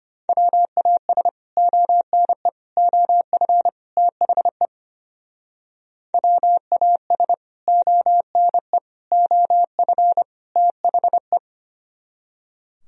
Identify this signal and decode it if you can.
Morse code, heard as a keyed tone, about 33 wpm, 700 Hz. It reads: WAS ONE OF THE WAS ONE OF THE